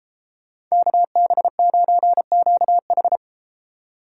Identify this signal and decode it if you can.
Morse code, heard as a keyed tone, about 33 wpm, 700 Hz. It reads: KB9QH